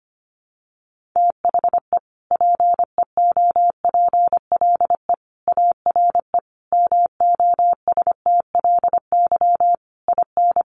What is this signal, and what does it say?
Morse code, heard as a keyed tone, about 25 wpm, 700 Hz. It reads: THE PEOPLE ARE MOSTLY IN
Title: Sentences from Top 1300 Words 25wpm